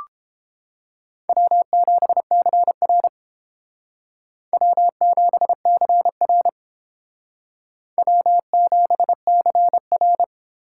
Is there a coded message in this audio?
Yes: Morse code from 1.3 to 10.2 s, heard as a keyed tone, about 28 wpm, 700 Hz. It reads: W7CR W7CR W7CR